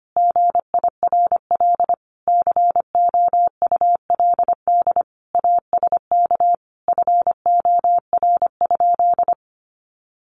Morse code, heard as a keyed tone, about 25 wpm, 700 Hz. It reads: GIRL COULD ASK FOR?
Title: Sentences from Top 1400 Words 25wpm